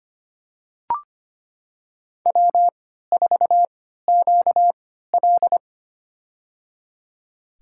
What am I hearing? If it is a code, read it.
Morse code, heard as a keyed tone, about 25 wpm, spaced as for 14 wpm, 700 Hz. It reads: W4QL
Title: USA Call Signs - 4 Characters - 3x Character Spacing - 25wpm